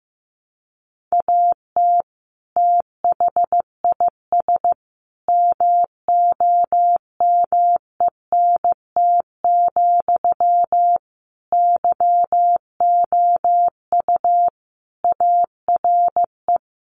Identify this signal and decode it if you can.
Morse code, heard as a keyed tone, about 15 wpm, 700 Hz. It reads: AT THIS MOMENT, YOU ARE
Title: Sentences from Top 1000 Words 15wpm